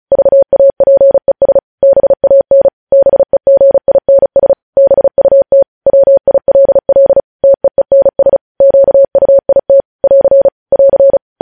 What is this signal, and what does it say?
Morse code, heard as a keyed tone, about 36 wpm, 550 Hz. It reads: VAPES BAN BEGINS BUT WILL TEENS QUIT <AR> <AR>
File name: podcast / News headlines-1x-35x35